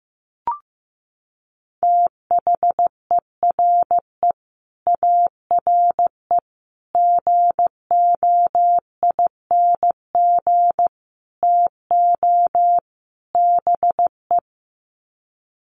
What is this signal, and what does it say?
Morse code, heard as a keyed tone, about 15 wpm, 700 Hz. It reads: THERE ARE GOING TO BE
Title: Sets of 5 Words 15wpm